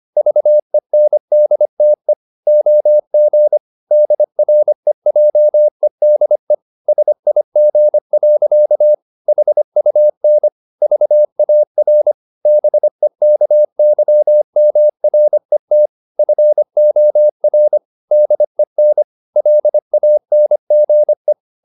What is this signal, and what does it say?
Morse code, heard as a keyed tone, about 25 wpm, 600 Hz. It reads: VENDTE OG DREJEDE SIG. HUN VAR BEKYMRET FOR DEN LANGE
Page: Ferie 25 wpm